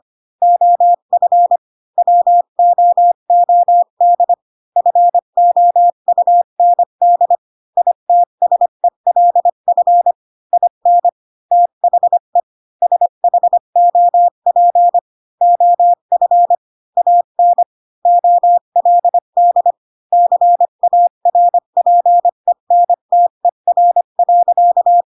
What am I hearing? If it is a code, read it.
Morse code, heard as a keyed tone, about 25 wpm, 700 Hz. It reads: OF WOOD FOUND ITSELF IN THE SHOP OF AN OLD CARPENTER.
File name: Pinocchio / Pinocchio ch1 20 25 ik1hge